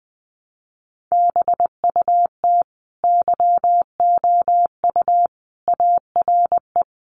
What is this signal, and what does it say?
Morse code, heard as a keyed tone, about 20 wpm, 700 Hz. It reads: BUT YOU ARE
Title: Sentences from Top 1300 Words 20wpm